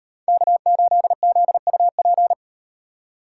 Morse code, heard as a keyed tone, about 38 wpm, 700 Hz. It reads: K8ZUP